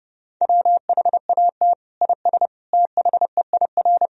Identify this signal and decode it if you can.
Morse code, heard as a keyed tone, about 32 wpm, 700 Hz. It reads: WHAT IS THEIR
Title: Sentences from Top 1000 Words - Review Part 1 - 30wpm